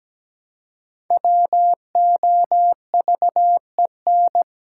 Morse code, heard as a keyed tone, about 17 wpm, 700 Hz. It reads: WOVEN
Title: Pangram Challenge v2 - 17wpm